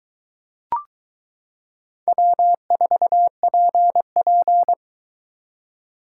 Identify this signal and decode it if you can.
Morse code, heard as a keyed tone, about 22 wpm, 700 Hz. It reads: W4PP